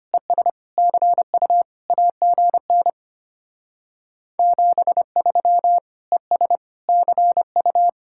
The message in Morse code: ES CU AGN 73 ES CU

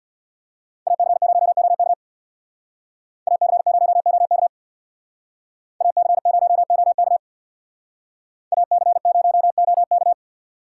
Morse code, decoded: AC0OK AC0OK AC0OK AC0OK

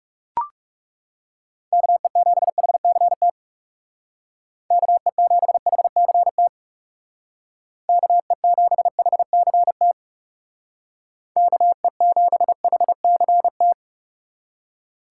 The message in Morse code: KE7HCT KE7HCT KE7HCT KE7HCT

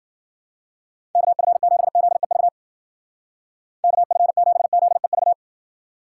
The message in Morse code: KF7ZEV KF7ZEV

